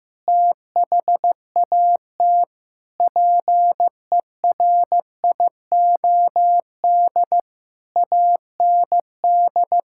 Morse code, heard as a keyed tone, about 15 wpm, 700 Hz. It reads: THAT PERIOD AND
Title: Sentences from Top 1000 Words - Review Part 5 - 15wpm